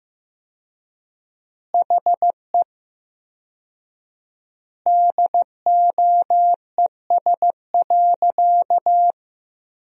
Morse code, heard as a keyed tone, about 16 wpm, 700 Hz. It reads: HE DOES.
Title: 4x Word-Spacing - Sentences from Top 500 Words - 15wpm